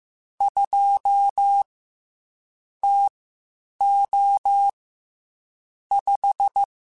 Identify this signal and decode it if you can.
Morse code, heard as a keyed tone, about 15 wpm, spaced as for 8 wpm, 800 Hz. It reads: 2 TO 5
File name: Spaced Characters / Spaced Characters-mp3s / Weather 5 12 sp @13